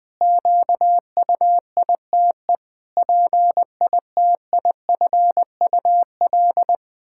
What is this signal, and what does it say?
Morse code, heard as a keyed tone, about 20 wpm, 700 Hz. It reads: QUITE PITIFUL
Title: Pangram Challenge v2 - 20wpm